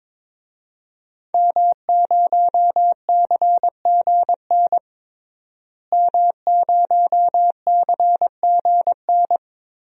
Morse code, heard as a keyed tone, about 22 wpm, 700 Hz. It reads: M0CGN M0CGN
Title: DXCC-100 Call Signs - 5 Characters - Repeated 3x - 22wpm